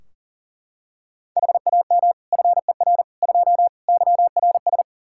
Morse code, heard as a keyed tone, about 40 wpm, 700 Hz. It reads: HAM FER 2 YRS
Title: Practice QSO - Exchange 3 - More Info - 40wpm